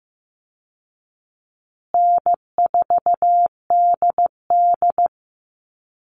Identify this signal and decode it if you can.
Morse code, heard as a keyed tone, about 15 wpm, 700 Hz. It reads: N4DD
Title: Easy Call Signs Repeated 3x - 15wpm